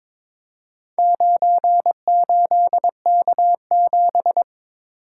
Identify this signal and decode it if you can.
Morse code, heard as a keyed tone, about 22 wpm, 700 Hz. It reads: 98K7
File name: practice / 22wpm / Four Letter-Number Combinations - 22wpm